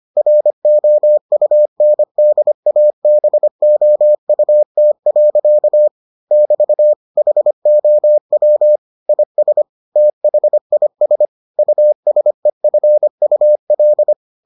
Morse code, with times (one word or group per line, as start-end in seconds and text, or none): 0.2-5.9 ROUNDABOUT.
6.3-6.9 <BT>
7.2-8.8 HOW
9.1-9.6 IS
10.0-11.3 THIS
11.6-14.1 USEFUL